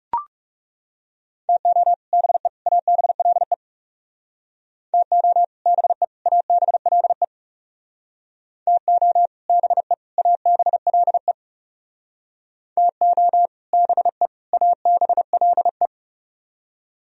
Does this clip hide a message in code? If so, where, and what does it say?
Yes: Morse code from 1.5 to 15.9 s, heard as a keyed tone, about 38 wpm, 700 Hz. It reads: TO BE ABLE TO BE ABLE TO BE ABLE TO BE ABLE